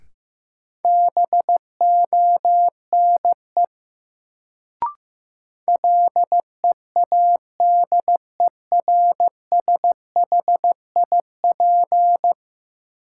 Morse code, heard as a keyed tone, about 15 wpm, 700 Hz. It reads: BONE LEADERSHIP
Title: Sentences from Top 1600 Words - New Word Drill Alt - 15wpm